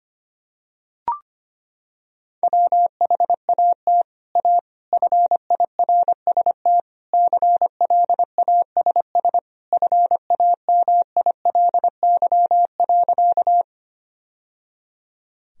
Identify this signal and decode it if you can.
Morse code, heard as a keyed tone, about 25 wpm, 700 Hz. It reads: WHAT A FIRST CLASS FAMILY.
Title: Sentences from Top 1000 Words - Review Part 2 - 25wpm